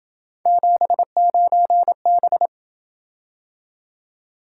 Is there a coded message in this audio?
Yes: Morse code from 0.5 to 2.5 s, heard as a keyed tone, about 27 wpm, 700 Hz. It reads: 79B